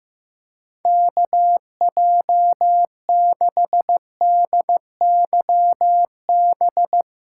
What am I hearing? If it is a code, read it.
Morse code, heard as a keyed tone, about 15 wpm, 700 Hz. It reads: KJ6DYB
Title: USA Callsign DE Callsign 15wpm